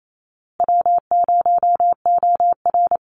Morse code, heard as a keyed tone, about 28 wpm, 700 Hz. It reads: W0OR
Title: Easy Call Signs Repeated 3x - 28wpm